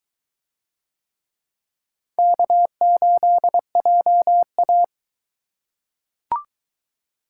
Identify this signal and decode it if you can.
Morse code, heard as a keyed tone, about 22 wpm, 700 Hz. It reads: K8JA